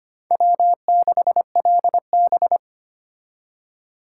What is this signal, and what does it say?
Morse code, heard as a keyed tone, about 33 wpm, 700 Hz. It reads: W6LB